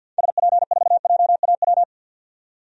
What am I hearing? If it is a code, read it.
Morse code, heard as a keyed tone, about 52 wpm, 700 Hz. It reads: SP4JAW
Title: DXCC-100 Call Signs - 6 Characters - 50wpm